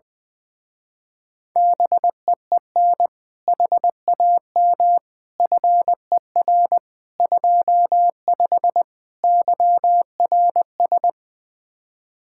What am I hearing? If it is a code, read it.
Morse code, heard as a keyed tone, about 20 wpm, 700 Hz. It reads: BEEN HAM FER 25 YRS
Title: Speed-Racing Common QSO Phrases - 20wpm